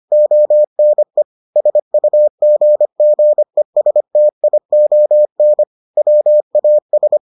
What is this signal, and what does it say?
Morse code, heard as a keyed tone, about 25 wpm, 600 Hz. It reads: ONE SUGGESTION WAS